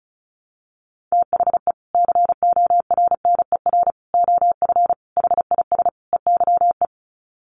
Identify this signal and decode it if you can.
Morse code, heard as a keyed tone, about 34 wpm, 700 Hz. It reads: THE CORNER OF HIS EYE
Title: Sets of 5 Words Encore2 35wpm